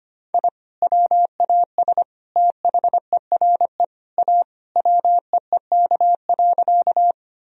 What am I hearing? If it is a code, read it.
Morse code, heard as a keyed tone, about 25 wpm, 700 Hz. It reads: I WAS THERE A WEEK.
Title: Sentences from Top 1000 Words - Review Part 2 - 25wpm